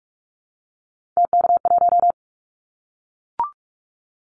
Morse code, heard as a keyed tone, about 44 wpm, 700 Hz. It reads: TK1
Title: Callsign Prefixes for Top 100 DXCC - 45wpm